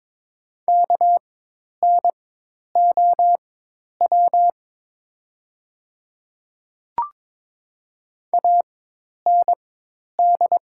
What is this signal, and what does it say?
Morse code, heard as a keyed tone, about 22 wpm, spaced as for 10 wpm, 700 Hz. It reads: KNOW AND
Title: 4x Character Spacing - Top 200 Words - 22wpm